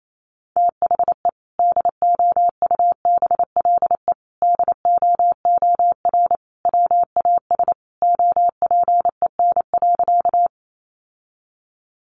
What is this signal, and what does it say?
Morse code, heard as a keyed tone, about 28 wpm, 700 Hz. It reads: THE DOUBLE DOOR WAS OPEN.
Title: Sentences from Top 1400 Words 28wpm